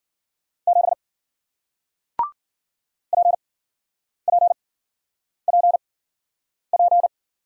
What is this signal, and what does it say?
Morse code, heard as a keyed tone, about 53 wpm, spaced as for 9 wpm, 700 Hz. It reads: 7 PPPP